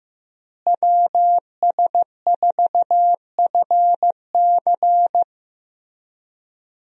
Morse code, heard as a keyed tone, about 15 wpm, 700 Hz. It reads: WS4FC
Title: USA Call Signs - 5 Characters - Repeated 3x - 15wpm